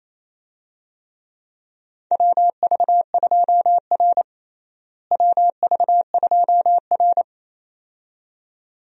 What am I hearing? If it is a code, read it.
Morse code, heard as a keyed tone, about 28 wpm, 700 Hz. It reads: WV2R WV2R
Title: USA Call Signs - 4 Characters - Repeated 2x - 28wpm